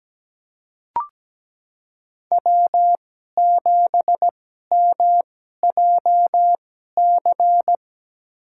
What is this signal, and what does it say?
Morse code, heard as a keyed tone, about 17 wpm, spaced as for 12 wpm, 700 Hz. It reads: W7MJC
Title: USA Call Signs - 2x Character Spacing - 17wpm